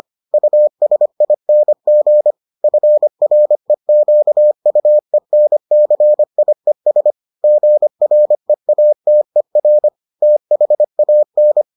Morse code, heard as a keyed tone, about 25 wpm, 600 Hz. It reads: USING FREQUENCIES GREATER THAN